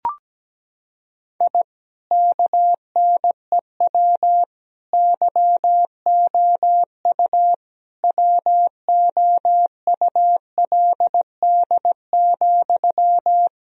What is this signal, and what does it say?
Morse code, heard as a keyed tone, about 17 wpm, 700 Hz. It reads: I KNEW YOU WOULD,
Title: Sentences from Top 1000 Words - Review Part 2 - 17wpm